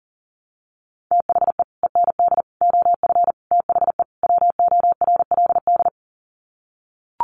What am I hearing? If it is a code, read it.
Morse code, heard as a keyed tone, about 40 wpm, 700 Hz. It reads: THE END OF THE WORLD
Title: Sets of 5 Words 40wpm